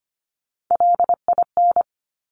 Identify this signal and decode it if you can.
Morse code, heard as a keyed tone, about 25 wpm, 700 Hz. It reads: LIN
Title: Top Three Letter Combinations in Words 25wpm